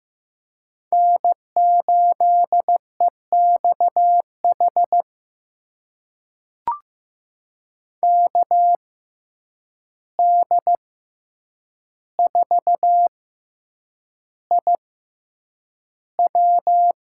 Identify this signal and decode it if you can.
Morse code, heard as a keyed tone, about 15 wpm, spaced as for 8 wpm, 700 Hz. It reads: N8EXH KD4IW
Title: USA Call Signs - 5 Characters - 6x Character Spacing - 15wpm